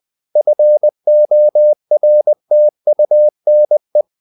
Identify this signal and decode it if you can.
Morse code, heard as a keyed tone, about 20 wpm, 600 Hz. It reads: FORTUNE
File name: CW / Practice / EnglishSayings20